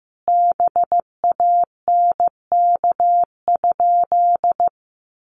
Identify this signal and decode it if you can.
Morse code, heard as a keyed tone, about 15 wpm, 700 Hz. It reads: BANK?
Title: Sentences from Top 1200 Words 15wpm